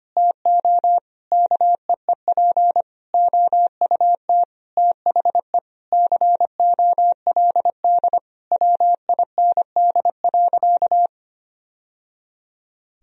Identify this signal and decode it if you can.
Morse code, heard as a keyed tone, about 25 wpm, 700 Hz. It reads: TO KEEP OUT THE COLD WIND.
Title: Sentences from Top 1000 Words - Review Part 2 - 25wpm